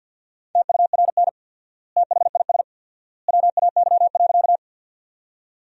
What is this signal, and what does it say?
Morse code, heard as a keyed tone, about 48 wpm, 700 Hz. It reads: TURN THIS WAY.
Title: 4x Word-Spacing - Sentences from Top 200 Words - 50wpm